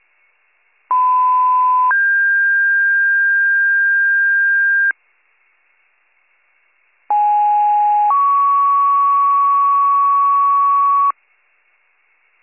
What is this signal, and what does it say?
Two-tone page, heard as a one-second tone, then a three-second tone, about 1010 then 1600 Hz, about 830 then 1120 Hz. It is heard from 0.9 to 11.1 s.